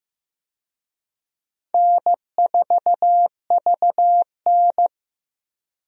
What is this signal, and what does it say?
Morse code, heard as a keyed tone, about 16 wpm, 700 Hz. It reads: N4VN
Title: USA Call Signs - 4 Characters - Repeated 2x - 15wpm